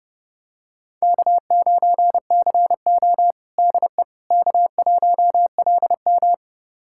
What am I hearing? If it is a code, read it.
Morse code, heard as a keyed tone, about 30 wpm, 700 Hz. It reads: K9CO DE K1LM
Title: USA Callsign DE Callsign - 4 Characters - 30wpm